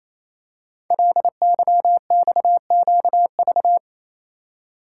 Morse code, heard as a keyed tone, about 29 wpm, 690 Hz. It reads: LYXQV
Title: Code Groups - 5 Letters - 28wpm